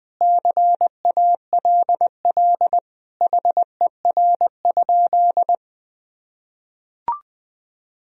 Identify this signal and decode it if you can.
Morse code, heard as a keyed tone, about 20 wpm, 700 Hz. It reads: CALL HER?
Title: Sentences from Top 100 Words and Abbreviations 20wpm